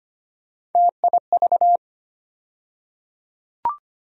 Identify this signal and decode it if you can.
Morse code, heard as a keyed tone, about 25 wpm, 700 Hz. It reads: TIV
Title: USA Call Sign Postfixes - 3 Letters 25wpm